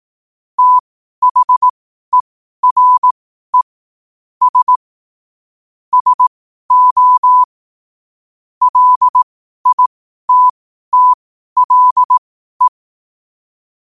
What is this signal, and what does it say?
Morse code, heard as a keyed tone, about 18 wpm, spaced as for 12 wpm, 1000 Hz. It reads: THERE S SO LITTLE
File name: qotdfiles / 12wpm